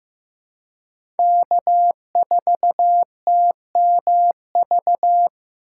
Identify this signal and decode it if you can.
Morse code, heard as a keyed tone, about 15 wpm, 700 Hz. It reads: K4TMV
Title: USA Call Signs - 5 Characters - 3x Character Spacing - 15wpm